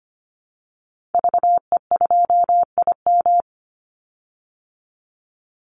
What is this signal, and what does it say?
Morse code, heard as a keyed tone, about 25 wpm, 700 Hz. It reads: VE2IM
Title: Call Signs Repeated 2x - 25wpm